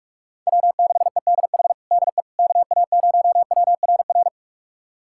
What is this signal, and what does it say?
Morse code, heard as a keyed tone, about 41 wpm, 700 Hz. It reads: W6EDH DE KA0WRR
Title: USA Callsign DE Callsign 45wpm